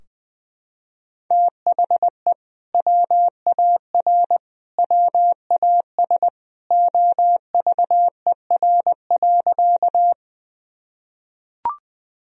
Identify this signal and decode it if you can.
Morse code, heard as a keyed tone, about 20 wpm, 700 Hz. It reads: THE WAR WAS OVER.
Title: Sentences from Top 1000 Words - Review Part 2 - 20wpm